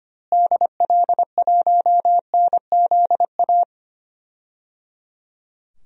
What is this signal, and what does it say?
Morse code, heard as a keyed tone, about 25 wpm, 700 Hz. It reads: DL1NZA
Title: DXCC-100 Call Signs - 6 Characters - Repeated 3x - 25wpm